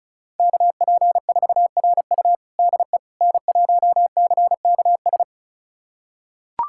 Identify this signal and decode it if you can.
Morse code, heard as a keyed tone, about 36 wpm, 700 Hz. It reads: KP4RU DE N1CKS
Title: USA Callsign DE Callsign - 5 Characters - 35wpm